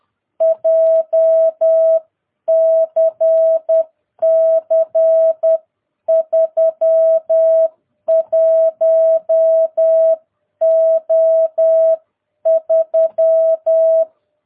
Morse code, heard as a keyed tone, about 10 wpm, 660 Hz. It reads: JCC 31O3
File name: jh8cbh / 2022cw